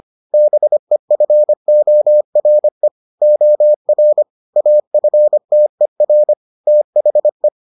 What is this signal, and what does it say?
Morse code, heard as a keyed tone, about 25 wpm, 600 Hz. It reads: BEFORE OR AFTER THE